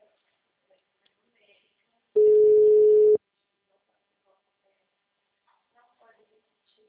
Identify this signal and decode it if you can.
Ringback tone (Europe, 425 Hz), with speech.